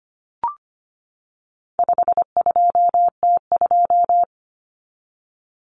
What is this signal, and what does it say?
Morse code, heard as a keyed tone, about 25 wpm, 700 Hz. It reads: 52T2
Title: Common Cut Numbers - 4-Digit 25wpm